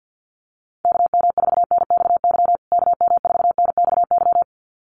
Morse code, heard as a keyed tone, about 50 wpm, 700 Hz. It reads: KM4NXY KM4NXY
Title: USA Call Signs Repeated 2x 50wpm